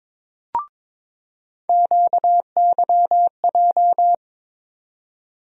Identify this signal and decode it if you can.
Morse code, heard as a keyed tone, about 22 wpm, 700 Hz. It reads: QYJ
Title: USA Call Sign Postfixes - 3 Letters 22wpm